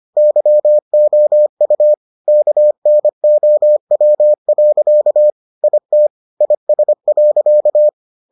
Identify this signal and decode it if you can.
Morse code, heard as a keyed tone, about 25 wpm, 600 Hz. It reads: YOU KNOW. IT IS.